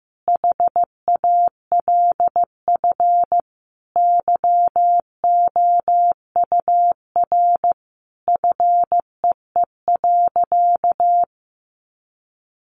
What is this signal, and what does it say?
Morse code, heard as a keyed tone, about 15 wpm, 700 Hz. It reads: HALF YOUR FEE.
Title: A Sound Of Thunder 15wpm